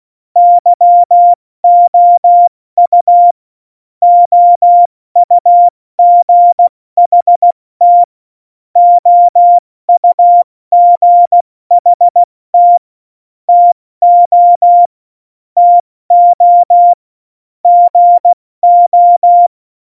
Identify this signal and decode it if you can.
Morse code, heard as a keyed tone, about 16 wpm, 700 Hz. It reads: YOU OUGHT OUGHT TO TO GO
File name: Morsekurs / Witze / Witz 25 Juli 17, Tempo 70 F80